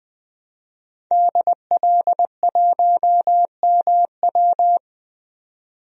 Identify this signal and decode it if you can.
Morse code, heard as a keyed tone, about 21 wpm, 700 Hz. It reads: DL1MW
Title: DXCC-100 Call Signs - 5 Characters - 20wpm